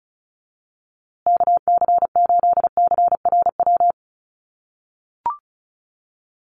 Morse code, heard as a keyed tone, about 34 wpm, 700 Hz. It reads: KC8CRW
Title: USA Call Signs Repeated 2x 35wpm